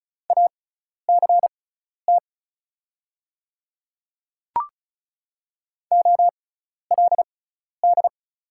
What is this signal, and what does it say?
Morse code, heard as a keyed tone, about 36 wpm, spaced as for 12 wpm, 690 Hz. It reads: ACT OLD
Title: 6x Character Spacing - Top 200 Words - 35wpm